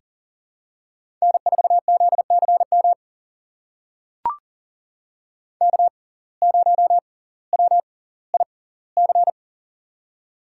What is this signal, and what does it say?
Morse code, heard as a keyed tone, about 37 wpm, spaced as for 20 wpm, 700 Hz. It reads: N4ZCM K0WIC
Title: USA Call Signs - 5 Characters - 6x Character Spacing - 40wpm